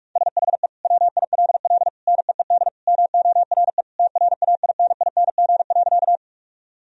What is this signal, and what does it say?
Morse code, heard as a keyed tone, about 46 wpm, 700 Hz. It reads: SHE WILL NEED MORE TRAINING.
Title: Sentences from Top 1000 Words - Review Part 4 - 45wpm